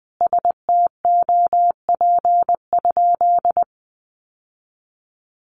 Morse code, heard as a keyed tone, about 20 wpm, 700 Hz. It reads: STOP?
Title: Sentences from Top 1300 Words 20wpm